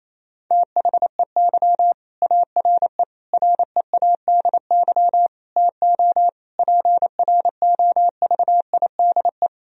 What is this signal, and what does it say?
Morse code, heard as a keyed tone, about 27 wpm, 700 Hz. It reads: THEY ARE READY TO PROVIDE
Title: Sentences from Top 1000 Words - Review Part 3 - 28wpm